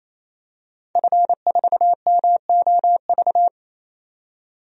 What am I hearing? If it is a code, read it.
Morse code, heard as a keyed tone, about 28 wpm, 700 Hz. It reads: F4MOV
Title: DXCC-100 Call Signs - 5 Characters - 28wpm